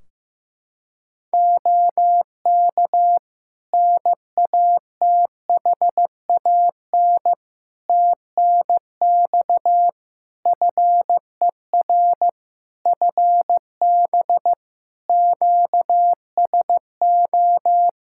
Morse code, heard as a keyed tone, about 15 wpm, 700 Hz. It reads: OK NATHAN TNX FER FB QSO
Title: Practice QSO - Exchange 4 - Ending - 15wpm